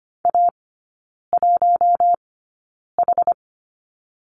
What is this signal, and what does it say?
Morse code, heard as a keyed tone, about 25 wpm, spaced as for 9 wpm, 690 Hz. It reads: A1H
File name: Lesson 14e - Number 2 - Three Characters 25wpm